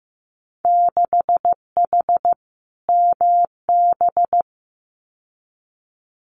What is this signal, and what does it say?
Morse code, heard as a keyed tone, about 15 wpm, 700 Hz. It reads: 6H MB
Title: Winter Field Day Exchanges Encore 15wpm